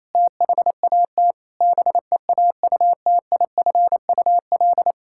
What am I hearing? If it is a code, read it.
Morse code, heard as a keyed tone, about 27 wpm, 700 Hz. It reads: THAT BEAUTIFUL
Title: Sentences from Top 1600 Words 28wpm